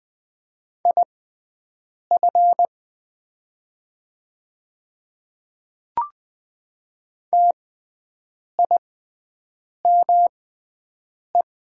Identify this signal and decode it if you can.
Morse code, heard as a keyed tone, about 20 wpm, spaced as for 7 wpm, 700 Hz. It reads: IF TIME
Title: 6x Character Spacing - Top 100 Words - 20wpm